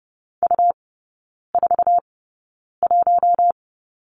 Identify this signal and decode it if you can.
Morse code, heard as a keyed tone, about 30 wpm, spaced as for 9 wpm, 700 Hz. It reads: U41
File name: Lesson 17e - Letter u - Three Characters 30wpm